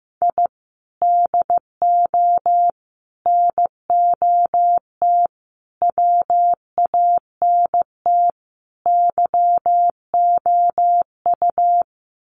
Morse code, heard as a keyed tone, about 15 wpm, 700 Hz. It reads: I DO NOT WANT YOU
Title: Sets of 5 Words 15wpm